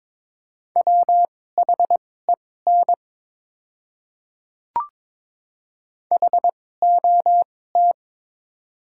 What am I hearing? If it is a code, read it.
Morse code, heard as a keyed tone, about 22 wpm, spaced as for 16 wpm, 700 Hz. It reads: WHEN HOT
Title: 2x Character Spacing - Top 100 Words - 22wpm